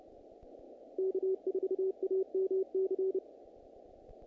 Morse code, heard as a keyed tone, about 30 wpm, 380 Hz. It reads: K4AMC
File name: contests / naqp-cw-k5tr-op-k5pi- -1900